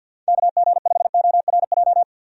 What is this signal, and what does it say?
Morse code, heard as a keyed tone, about 52 wpm, 700 Hz. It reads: KG5ORJ
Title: USA Call Signs - 6 Characters - 50wpm